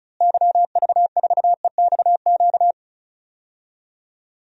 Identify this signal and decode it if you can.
Morse code, heard as a keyed tone, about 34 wpm, 700 Hz. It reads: YV4EXQ
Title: DXCC-100 Call Signs - 6 Characters - Repeated 2x - 35wpm